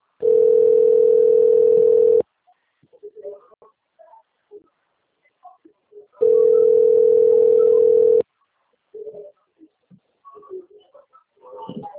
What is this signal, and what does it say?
Ringback tone (North America), 440 + 480 Hz.